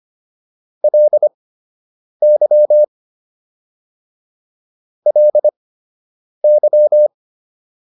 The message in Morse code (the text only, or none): LY LY